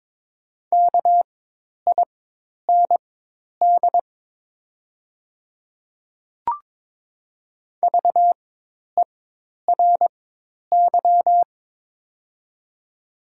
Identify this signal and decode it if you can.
Morse code, heard as a keyed tone, about 23 wpm, spaced as for 10 wpm, 700 Hz. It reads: KIND VERY
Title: 4x Character Spacing - Top 200 Words - 22wpm